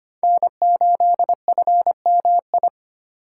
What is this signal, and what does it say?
Morse code, heard as a keyed tone, about 25 wpm, 700 Hz. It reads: N8FMI
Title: USA Call Signs - 5 Characters - 3x Character Spacing - 25wpm